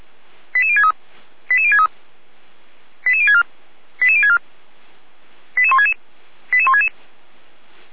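Five-tone selective call (ZVEI), heard as quick runs of five tones, tones of about 70 ms. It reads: 88063 88063 88064 88064 88166 88166
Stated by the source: SDR-received ZVEI five-tone call 88303.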